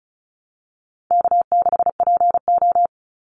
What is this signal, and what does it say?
Morse code, heard as a keyed tone, about 34 wpm, 700 Hz. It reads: K6PO
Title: Call Signs Repeated 2x - 35wpm